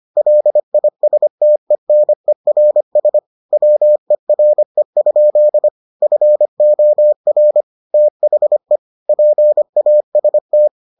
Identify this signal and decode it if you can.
Morse code, heard as a keyed tone, about 25 wpm, 600 Hz. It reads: LISTENERS WERE? FOR THE PAST